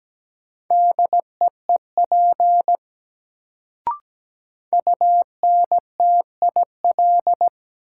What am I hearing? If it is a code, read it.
Morse code, heard as a keyed tone, about 17 wpm, 700 Hz. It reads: DEEP UNTIL
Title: Sets of 5 Words Encore - New Word Drill - 17wpm